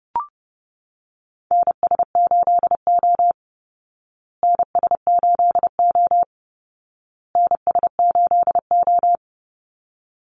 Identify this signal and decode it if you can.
Morse code, heard as a keyed tone, about 28 wpm, 700 Hz. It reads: NS8O NS8O NS8O
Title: Call Signs Repeated 2x - 30wpm